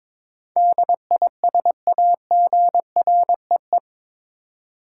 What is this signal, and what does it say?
Morse code, heard as a keyed tone, about 22 wpm, 700 Hz. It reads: DISAGREE
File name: practice / 22wpm / Eight Letter Words - 22wpm